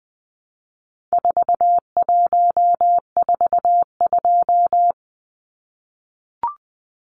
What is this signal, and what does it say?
Morse code, heard as a keyed tone, about 20 wpm, 700 Hz. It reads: 4142
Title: Numbers - 4-Digit 20wpm